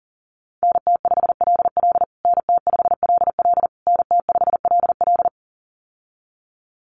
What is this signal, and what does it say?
Morse code, heard as a keyed tone, about 40 wpm, 700 Hz. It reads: NT5LL NT5LL NT5LL
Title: USA Call Signs Repeated 3x 40wpm